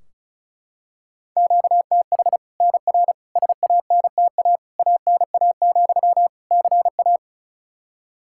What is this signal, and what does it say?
Morse code, heard as a keyed tone, about 34 wpm, 700 Hz. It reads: QTH NR SANTA ANA, CA
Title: Practice QSO - Exchange 1 - Beginning - 35wpm